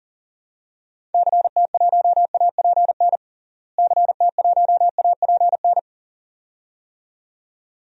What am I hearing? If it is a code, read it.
Morse code, heard as a keyed tone, about 37 wpm, 700 Hz. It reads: CT1APN CT1APN
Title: DXCC-100 Call Signs - 6 Characters - Repeated 2x - 40wpm